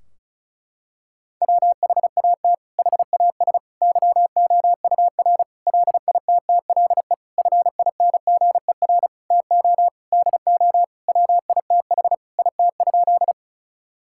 Morse code, heard as a keyed tone, about 36 wpm, 700 Hz. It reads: WHAT HAS YOUR LITTLE FINGER TO DO WITH IT?
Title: Sentences from Top 1600 Words 35wpm